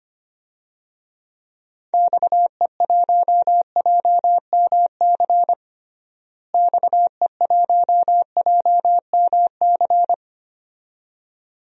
Morse code, heard as a keyed tone, about 25 wpm, 700 Hz. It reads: XE1JMC XE1JMC
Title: DXCC-100 Call Signs - 6 Characters - Repeated 2x - 25wpm